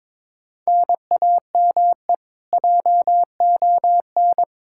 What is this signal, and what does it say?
Morse code, heard as a keyed tone, about 22 wpm, 700 Hz. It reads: NAME JON
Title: K1USN SST QSO Practice 22wpm